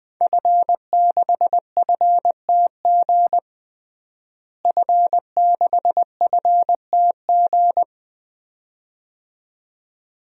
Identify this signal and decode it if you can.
Morse code, heard as a keyed tone, about 21 wpm, 700 Hz. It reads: F6FTG F6FTG
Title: DXCC-100 Call Signs - 5 Characters - Repeated 2x - 20wpm